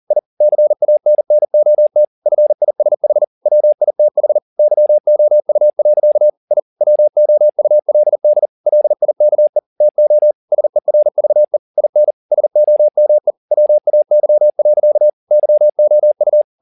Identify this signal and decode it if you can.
Morse code, heard as a keyed tone, about 40 wpm, 600 Hz. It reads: I CANNOT FISH WITH YOU. I WOULD LIKE TO SERVE IN SOME WAY. YOU
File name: the old man and the sea 1 40wpm